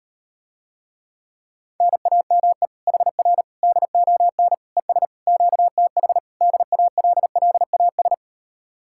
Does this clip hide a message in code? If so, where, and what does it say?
Yes: Morse code from 1.8 to 8.1 s, heard as a keyed tone, about 39 wpm, 700 Hz. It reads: NAME HR DON ES QTH DALLAS